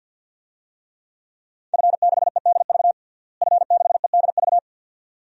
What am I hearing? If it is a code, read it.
Morse code, heard as a keyed tone, about 52 wpm, 700 Hz. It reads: F6EDV F6EDV
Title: DXCC-100 Call Signs - 5 Characters - Repeated 2x - 50wpm